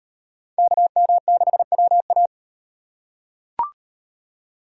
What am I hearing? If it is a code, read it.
Morse code, heard as a keyed tone, about 38 wpm, 700 Hz. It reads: KM6WA